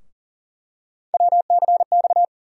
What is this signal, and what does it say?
Morse code, heard as a keyed tone, about 38 wpm, 700 Hz. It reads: WCX
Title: USA Call Sign Postfixes - 3 Letters 40wpm